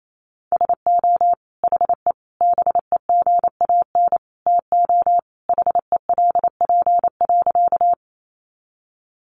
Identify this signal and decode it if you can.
Morse code, heard as a keyed tone, about 28 wpm, 700 Hz. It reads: SO HE BEGAN TO HELP.
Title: Sentences from Top 300 Words 28wpm